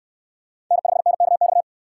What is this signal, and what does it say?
Morse code, heard as a keyed tone, about 69 wpm, 700 Hz. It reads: N5NKX